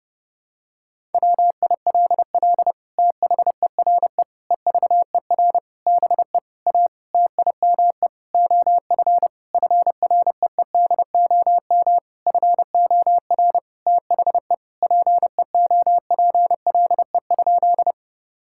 Morse code, heard as a keyed tone, about 32 wpm, 700 Hz. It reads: WILL THERE EVER BE A TIME OF FREEDOM FOR THE PEOPLE?
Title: Sentences from Top 1000 Words - Review Part 5 - 30wpm